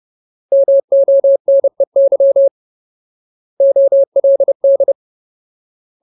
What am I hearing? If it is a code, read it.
Morse code, heard as a keyed tone, about 30 wpm, 550 Hz. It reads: MONEY OLD
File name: Ultimate Word List / 30WPM Ultimate